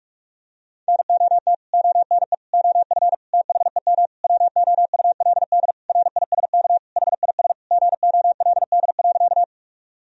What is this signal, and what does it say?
Morse code, heard as a keyed tone, about 46 wpm, 700 Hz. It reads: NOT ONE OF THEM WOULD RISK HIS GOLD.
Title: Sentences from Top 1000 Words - Review Part 4 - 45wpm